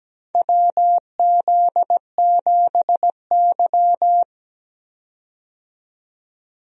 Morse code, heard as a keyed tone, about 17 wpm, 700 Hz. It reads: WZ7Y
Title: USA Call Signs - 4 Characters - 17wpm